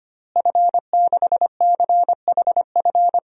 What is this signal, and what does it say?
Morse code, heard as a keyed tone, about 25 wpm, 700 Hz. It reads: F6CHF
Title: DXCC-100 Call Signs - 5 Characters - Repeated 3x - 25wpm